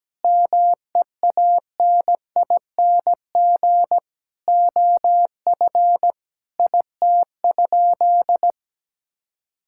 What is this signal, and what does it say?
Morse code, heard as a keyed tone, about 17 wpm, 700 Hz. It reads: MEANING OF IT?
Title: Sentences from Top 1000 Words - Review Part 6 - 17wpm